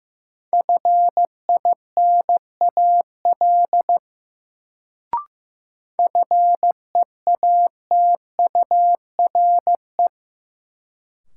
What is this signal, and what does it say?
Morse code, heard as a keyed tone, about 15 wpm, 700 Hz. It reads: FINAL FEATURE
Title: Sentences from Top 800 Words - New Word Drill - 15wpm